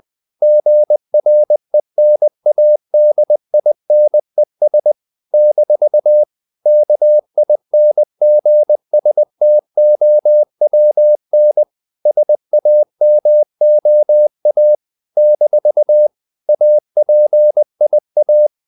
Morse code, heard as a keyed tone, about 20 wpm, 600 Hz. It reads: GRENADINES - KINGSTOWN SAMOA - APIA
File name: CW / Practice / NationalCapitals20